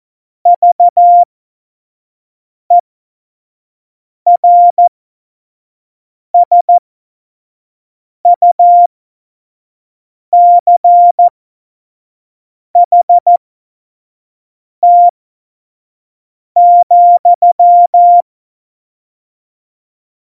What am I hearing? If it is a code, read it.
Morse code, heard as a keyed tone, about 14 wpm, spaced as for 5 wpm, 700 Hz. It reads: VERSUCHT,
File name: Morsekurs / Witze / Witz 26. Februar 2019, 25 F70